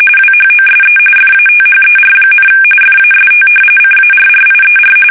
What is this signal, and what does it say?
RTTY, about 45.45 baud. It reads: WHOSE FATHER, FAST CARRIER
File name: fsk850 / battleshipnj